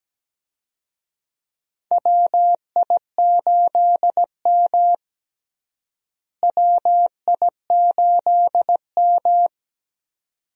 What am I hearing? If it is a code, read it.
Morse code, heard as a keyed tone, about 17 wpm, 700 Hz. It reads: WI8M WI8M
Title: USA Call Signs - 4 Characters - Repeated 2x - 17wpm